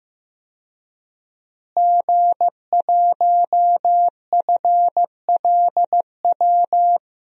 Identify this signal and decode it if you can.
Morse code, heard as a keyed tone, about 15 wpm, 700 Hz. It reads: G1FLW
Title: DXCC-100 Call Signs - 5 Characters - Repeated 3x - 15wpm